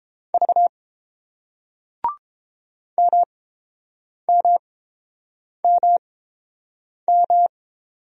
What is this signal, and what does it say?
Morse code, heard as a keyed tone, about 31 wpm, spaced as for 7 wpm, 700 Hz. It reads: V MMMM